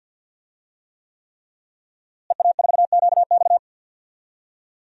Morse code, heard as a keyed tone, about 50 wpm, 700 Hz. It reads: EA4QX
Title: DXCC-100 Call Signs - 5 Characters - Repeated 3x - 50wpm